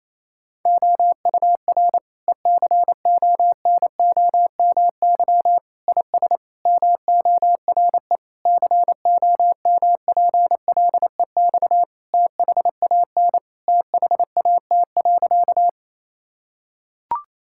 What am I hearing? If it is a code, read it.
Morse code, heard as a keyed tone, about 27 wpm, 700 Hz. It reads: OUR ECONOMY IS MORE COMPLEX THAN THAT.
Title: Sentences from Top 1000 Words - Review Part 6 - 28wpm